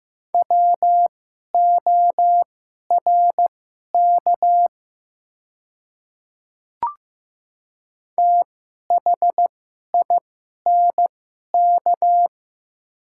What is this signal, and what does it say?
Morse code, heard as a keyed tone, about 15 wpm, spaced as for 11 wpm, 700 Hz. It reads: WORK THINK
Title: 2x Character Spacing - Top 500 Words - 15wpm